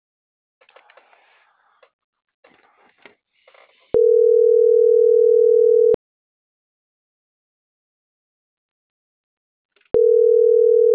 Ringback tone (North America), 440 + 480 Hz.